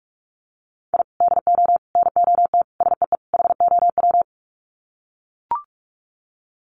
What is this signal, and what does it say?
Morse code, heard as a keyed tone, about 44 wpm, 700 Hz. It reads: I DO NOT SEE HOW
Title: Sets of 5 Words 45wpm